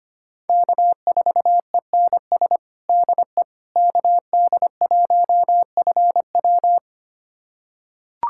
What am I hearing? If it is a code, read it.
Morse code, heard as a keyed tone, about 25 wpm, 700 Hz. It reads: K4ENS DE KD1FW
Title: USA Callsign DE Callsign 25wpm